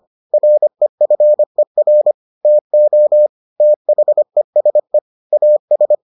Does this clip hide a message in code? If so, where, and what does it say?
Yes: Morse code from 0.3 to 6.0 s, heard as a keyed tone, about 25 wpm, 600 Hz. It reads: REFER TO THESE AS